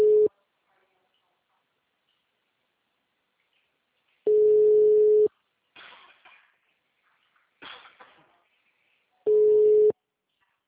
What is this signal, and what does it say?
Ringback tone (Europe, 425 Hz), with speech.
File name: out- -2053- - - .221